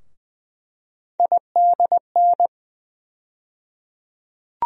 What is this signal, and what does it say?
Morse code, heard as a keyed tone, about 20 wpm, 700 Hz. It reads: IDN
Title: USA Call Sign Postfixes - 3 Letters 20wpm